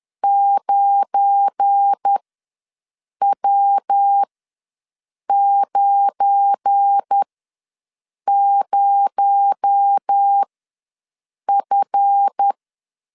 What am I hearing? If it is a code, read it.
Morse code, heard as a keyed tone, about 10 wpm, spaced as for 6 wpm, 800 Hz. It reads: 9W90F
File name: LEZIONI / 7 GF9Z / 7 GF9 ws 10wpm 4034